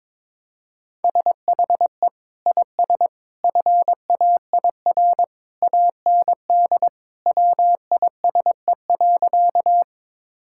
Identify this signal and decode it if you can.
Morse code, heard as a keyed tone, about 22 wpm, 700 Hz. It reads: SHE IS FAIR AND WISE.
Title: Sentences from Top 1600 Words 22wpm